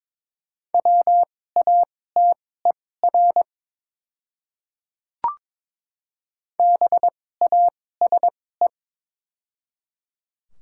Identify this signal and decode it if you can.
Morse code, heard as a keyed tone, about 22 wpm, spaced as for 16 wpm, 700 Hz. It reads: WATER BASE
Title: 2x Character Spacing - Top 300 Words - 22wpm